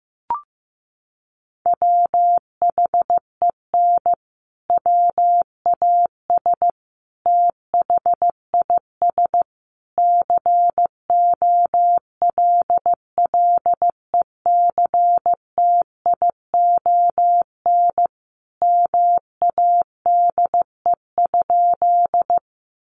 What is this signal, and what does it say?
Morse code, heard as a keyed tone, about 15 wpm, 700 Hz. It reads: WHEN WAS THIS COLLECTION MADE?
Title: Sentences from Top 1200 Words 15wpm